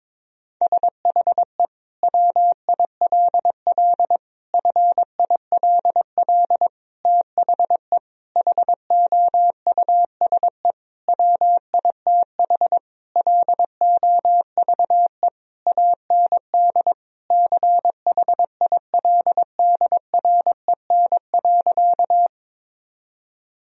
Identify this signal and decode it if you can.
Morse code, heard as a keyed tone, about 22 wpm, 700 Hz. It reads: SHE WILL FILL THE HOUSE WITH LOVE AND CHILDREN.
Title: Sentences from Top 1000 Words - Review Part 3 - 22wpm